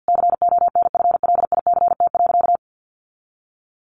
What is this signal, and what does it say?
Morse code, heard as a keyed tone, about 48 wpm, 700 Hz. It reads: CONFLICT.
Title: Sentences from Top 1300 Words 50wpm